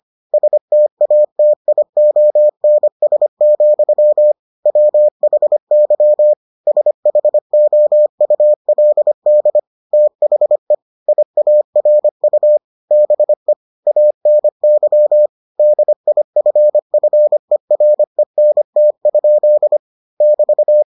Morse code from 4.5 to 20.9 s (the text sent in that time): WHY SHOULD THE IARU BE ANY DIFFERENT? <BT>